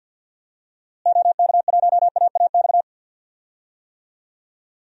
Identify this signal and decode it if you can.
Morse code, heard as a keyed tone, about 46 wpm, 700 Hz. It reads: OK1AAX
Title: DXCC-100 Call Signs - 6 Characters - Repeated 2x - 50wpm